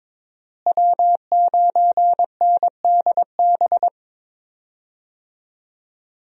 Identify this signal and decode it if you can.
Morse code, heard as a keyed tone, about 22 wpm, 700 Hz. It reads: W9NDB
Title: USA Call Signs - 5 Characters - Repeated 3x - 22wpm